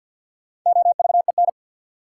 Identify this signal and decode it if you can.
Morse code, heard as a keyed tone, about 52 wpm, 700 Hz. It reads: OVEN
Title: Sentences from Top 1000 Words - Review Part 4 - New Word Drill - 50wpm